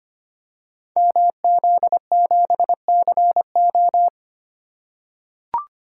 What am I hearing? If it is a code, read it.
Morse code, heard as a keyed tone, about 25 wpm, 700 Hz. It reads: MZ7CO
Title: Code Groups - 5 Letter:Numbers - 25wpm